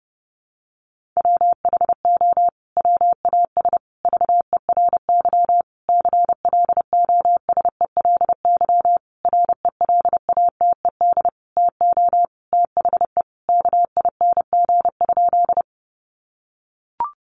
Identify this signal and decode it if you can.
Morse code, heard as a keyed tone, about 30 wpm, 700 Hz. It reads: WHO WAS VERY CLOSELY RELATED TO THE KING?
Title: Sentences from Top 1400 Words 30wpm